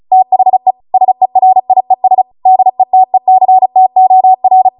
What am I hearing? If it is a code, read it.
Morse code, heard as a keyed tone, about 36 wpm, 750 Hz. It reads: THE SERIES DETECTOR